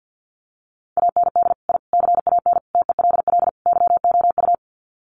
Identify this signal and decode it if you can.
Morse code, heard as a keyed tone, about 50 wpm, 700 Hz. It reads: AND I CAN TELL YOU
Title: Sets of 5 Words Encore2 50wpm